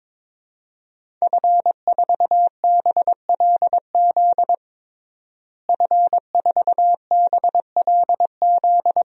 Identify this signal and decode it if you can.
Morse code, heard as a keyed tone, about 22 wpm, 700 Hz. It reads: F4BLZ F4BLZ
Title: DXCC-100 Call Signs - 5 Characters - Repeated 3x - 22wpm